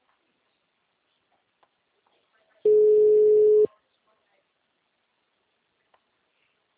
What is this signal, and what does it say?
Ringback tone (Europe, 425 Hz), with speech.
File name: out- -2040- - - .27985